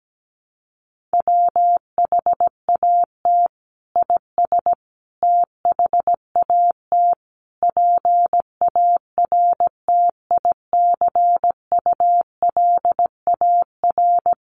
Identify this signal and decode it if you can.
Morse code, heard as a keyed tone, about 17 wpm, 700 Hz. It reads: WHAT IS THAT PARTICULAR
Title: Sentences from Top 1500 Words 17wpm